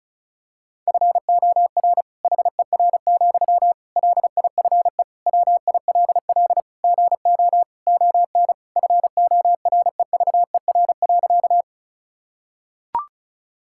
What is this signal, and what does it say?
Morse code, heard as a keyed tone, about 36 wpm, 700 Hz. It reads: FOR HER, LIFE WILL GO ON FOREVER.
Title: Sentences from Top 1600 Words 35wpm